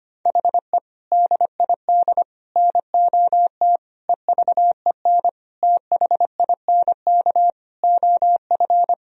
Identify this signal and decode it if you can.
Morse code, heard as a keyed tone, about 25 wpm, 700 Hz. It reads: HE DID NOT EVEN THINK OF
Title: Sentences from Top 1000 Words - Review Part 1 - 25wpm